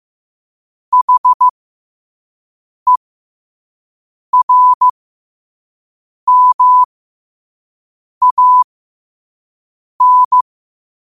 Morse code, heard as a keyed tone, about 15 wpm, spaced as for 5 wpm, 1000 Hz. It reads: HERMAN
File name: qotdfiles / 5wpm